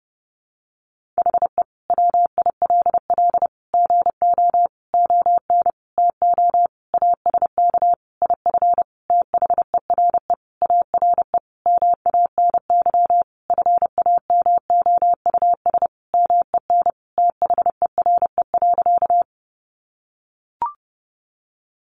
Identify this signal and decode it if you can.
Morse code, heard as a keyed tone, about 30 wpm, 700 Hz. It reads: HE WILL GO ON TO ASK IF THERE ARE MANY FAMOUS MEN THERE.
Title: Sentences from Top 1500 Words 30wpm